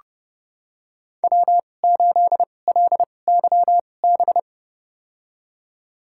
Morse code, heard as a keyed tone, about 30 wpm, spaced as for 22 wpm, 700 Hz. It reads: W8LYB
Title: USA Call Signs - 2x Character Spacing - 30wpm